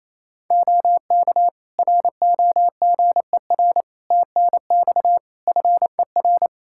Morse code, heard as a keyed tone, about 28 wpm, 700 Hz. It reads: OK ROGER TNX FER
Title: Practice QSO - Exchange 4 - Ending - 28wpm